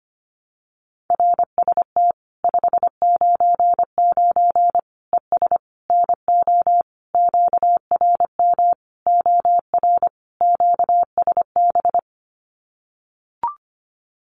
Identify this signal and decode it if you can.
Morse code, heard as a keyed tone, about 25 wpm, 700 Hz. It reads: RST 599 ES NO QRM OR QSB
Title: Common QSO Phrases 25wpm